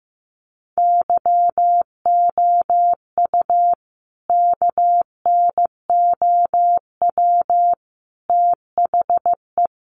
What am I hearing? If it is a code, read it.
Morse code, heard as a keyed tone, about 15 wpm, 700 Hz. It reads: YOU KNOW THE
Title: Sentences from Top 1000 Words 15wpm